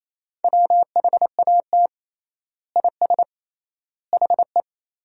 Morse code, heard as a keyed tone, about 28 wpm, 700 Hz. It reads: WHAT IS HE
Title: 3x Word-Spacing - Sentences from Top 100 Words - 28wpm